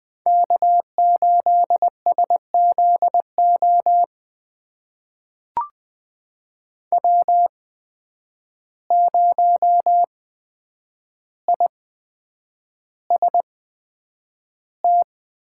Morse code, heard as a keyed tone, about 20 wpm, spaced as for 9 wpm, 700 Hz. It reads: K8SZO W0IST
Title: USA Call Signs - 5 Characters - 8x Character Spacing - 20wpm